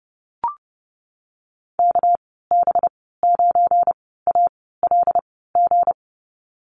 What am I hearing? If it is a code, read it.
Morse code, heard as a keyed tone, about 32 wpm, spaced as for 17 wpm, 700 Hz. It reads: KB9ALG
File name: Lesson 37g - Letter x - Call Signs 30wpm